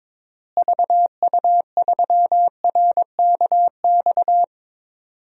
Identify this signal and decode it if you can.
Morse code, heard as a keyed tone, about 22 wpm, 700 Hz. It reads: VU3RKX
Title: DXCC-100 Call Signs - 6 Characters - Repeated 3x - 22wpm